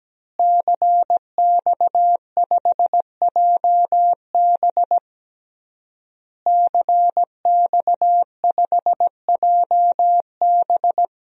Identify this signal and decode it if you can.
Morse code, heard as a keyed tone, about 17 wpm, 700 Hz. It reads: CX5JB CX5JB
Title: DXCC-100 Call Signs - 5 Characters - Repeated 3x - 17wpm